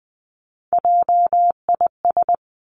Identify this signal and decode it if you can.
Morse code, heard as a keyed tone, about 20 wpm, 700 Hz. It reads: JIS
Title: Three Challenging Letter-Number Combinations 20wpm